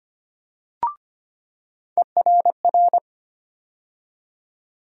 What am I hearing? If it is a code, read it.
Morse code, heard as a keyed tone, about 25 wpm, 700 Hz. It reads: ERR
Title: USA Call Sign Postfixes - 3 Letters 25wpm